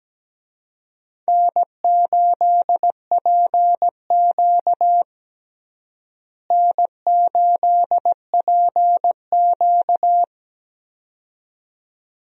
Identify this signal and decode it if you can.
Morse code, heard as a keyed tone, about 17 wpm, 700 Hz. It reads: N8PQ N8PQ
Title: USA Call Signs - 4 Characters - Repeated 2x - 17wpm